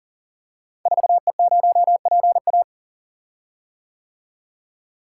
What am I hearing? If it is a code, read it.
Morse code, heard as a keyed tone, about 39 wpm, 700 Hz. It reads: 4E0PA
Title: Code Groups - 5 Letter:Numbers - 40wpm